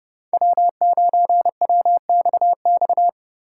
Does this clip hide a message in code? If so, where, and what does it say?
Yes: Morse code from 0.3 to 3.1 s, heard as a keyed tone, about 27 wpm, 700 Hz. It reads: W9WXX